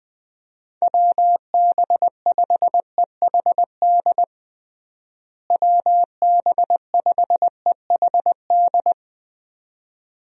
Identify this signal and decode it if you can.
Morse code, heard as a keyed tone, about 20 wpm, 700 Hz. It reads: WB5EHD WB5EHD
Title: USA Call Signs - 6 Characters - Repeated 2x - 20wpm